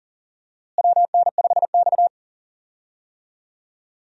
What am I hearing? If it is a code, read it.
Morse code, heard as a keyed tone, about 40 wpm, 700 Hz. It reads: WN5X
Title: USA Call Signs - 4 Characters - Repeated 3x - 40wpm